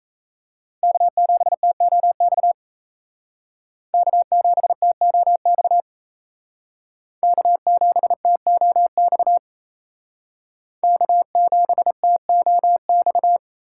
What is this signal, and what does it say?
Morse code, heard as a keyed tone, about 35 wpm, 700 Hz. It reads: K7TOX K7TOX K7TOX K7TOX
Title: Speed Racing - USA 1x3 Call Signs 28wpm